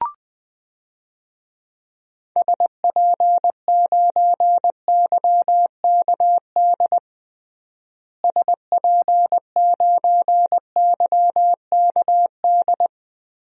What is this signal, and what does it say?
Morse code, heard as a keyed tone, about 19 wpm, 700 Hz. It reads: SP9YKD SP9YKD
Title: DXCC-100 Call Signs - 6 Characters - Repeated 2x - 20wpm